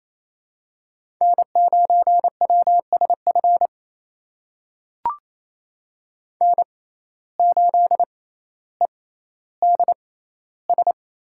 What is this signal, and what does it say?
Morse code, heard as a keyed tone, about 28 wpm, spaced as for 14 wpm, 700 Hz. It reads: N9WSF N8EDS
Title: USA Call Signs - 5 Characters - 6x Character Spacing - 28wpm